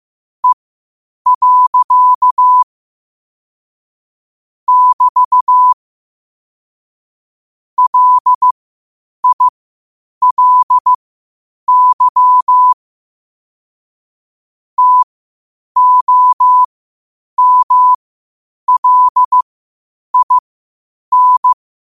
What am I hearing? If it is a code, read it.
Morse code, heard as a keyed tone, about 15 wpm, spaced as for 8 wpm, 1000 Hz. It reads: E. <BT> LILY TOMLIN
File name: qotdfiles / 8wpm